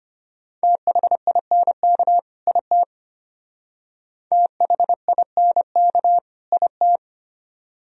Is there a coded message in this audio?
Yes: Morse code from 0.6 to 7.0 s, heard as a keyed tone, about 32 wpm, 700 Hz. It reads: THINK IT THINK IT